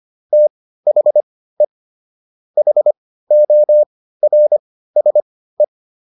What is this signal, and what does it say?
Morse code, heard as a keyed tone, about 25 wpm, spaced as for 15 wpm, 600 Hz. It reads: THE HORSE